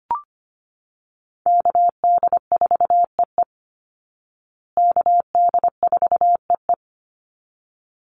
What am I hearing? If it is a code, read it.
Morse code, heard as a keyed tone, about 25 wpm, 700 Hz. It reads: KD4EE KD4EE
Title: Call Signs Repeated 3x - 25wpm